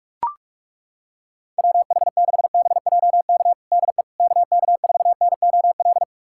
Morse code, heard as a keyed tone, about 42 wpm, 690 Hz. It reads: WH6BJK DE KK4NOL
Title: USA Callsign DE Callsign 45wpm